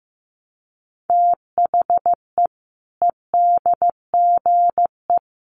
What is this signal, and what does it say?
Morse code, heard as a keyed tone, about 15 wpm, 700 Hz. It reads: THE EDGE
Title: Sets of 3 Words 15wpm